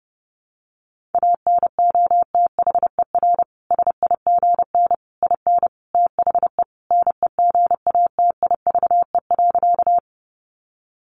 Morse code, heard as a keyed tone, about 30 wpm, 700 Hz. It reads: ANOTHER SIGN IN THE NEGATIVE.
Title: Sentences from Top 1000 Words 30wpm